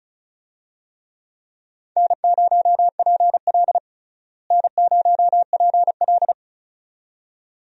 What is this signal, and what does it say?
Morse code, heard as a keyed tone, about 36 wpm, 700 Hz. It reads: N0PL N0PL
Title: USA Call Signs - 4 Characters - Repeated 2x - 35wpm